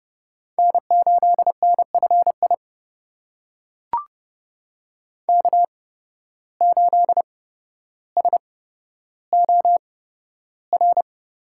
Morse code, heard as a keyed tone, about 31 wpm, spaced as for 13 wpm, 700 Hz. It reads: N8NFI K8SOR
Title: USA Call Signs - 5 Characters - 8x Character Spacing - 30wpm